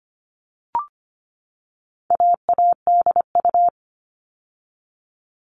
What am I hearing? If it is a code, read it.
Morse code, heard as a keyed tone, about 25 wpm, 700 Hz. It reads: AADU
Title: All Cut Numbers - 4-Digit 25wpm